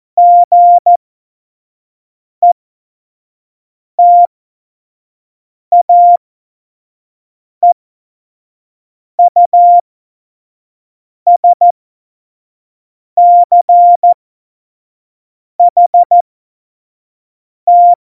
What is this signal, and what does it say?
Morse code, heard as a keyed tone, about 14 wpm, spaced as for 5 wpm, 700 Hz. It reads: GETAEUSCHT
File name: Morsekurs / Witze / Witz 25. Juni 2019, 25 F70